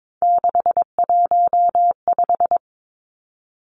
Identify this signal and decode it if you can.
Morse code, heard as a keyed tone, about 22 wpm, 700 Hz. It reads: 615
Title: Three Challenging Letter-Number Combinations 22wpm